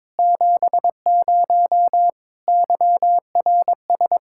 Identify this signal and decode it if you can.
Morse code, heard as a keyed tone, about 22 wpm, 700 Hz. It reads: 70 YRS
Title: Practice QSO - Exchange 3 - More Info - 22wpm